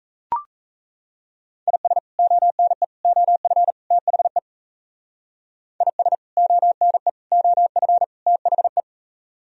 Morse code, heard as a keyed tone, about 35 wpm, 700 Hz. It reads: IS ONE OF THE IS ONE OF THE